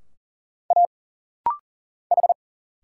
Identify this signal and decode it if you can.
Morse code, heard as a keyed tone, about 40 wpm, 700 Hz. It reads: A H
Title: Single Letters - Rapid-Fire 40wpm